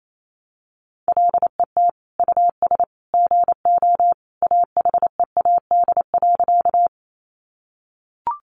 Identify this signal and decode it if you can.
Morse code, heard as a keyed tone, about 28 wpm, 700 Hz. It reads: LET US GO AHEAD.
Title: Sentences from Top 1100 Words 28wpm